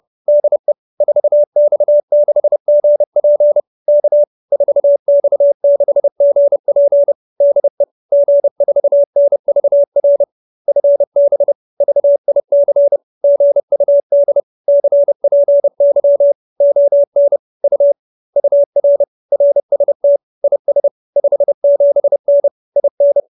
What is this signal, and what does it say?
Morse code, heard as a keyed tone, about 30 wpm, 570 Hz. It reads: DE 4X6GP K 4X6GP DE G4NVR FB VIC GUD CPY ON U UR RST IS 57N IN
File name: QSO315 30